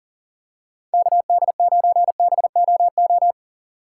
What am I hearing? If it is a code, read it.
Morse code, heard as a keyed tone, about 40 wpm, 700 Hz. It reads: KD9BOO
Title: USA Call Signs - 6 Characters - 2x Character Spacing - 40wpm